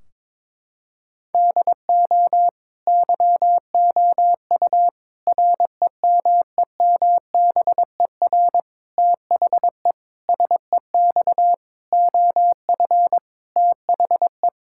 Morse code, heard as a keyed tone, about 22 wpm, 700 Hz. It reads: DO YOU REMEMBER THE SEX OF THE
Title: Sentences from Top 1600 Words 22wpm